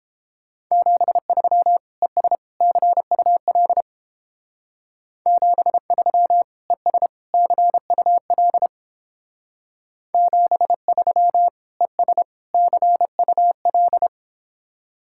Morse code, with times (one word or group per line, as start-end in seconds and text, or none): 0.7-1.8 73
2.0-2.4 ES
2.6-3.8 CUL
5.3-6.4 73
6.7-7.1 ES
7.3-8.7 CUL
10.1-11.5 73
11.8-12.2 ES
12.5-14.1 CUL